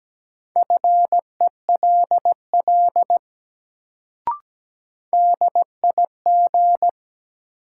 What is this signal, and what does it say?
Morse code, heard as a keyed tone, about 17 wpm, 700 Hz. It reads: FELL DIG
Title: Sentences from Top 1600 Words - New Word Drill - 17wpm